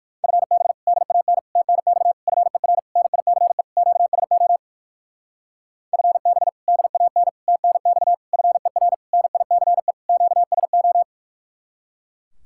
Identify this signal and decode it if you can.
Morse code, heard as a keyed tone, about 59 wpm, 700 Hz. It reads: FB DAN TNX FER NICE QSO FB DAN TNX FER NICE QSO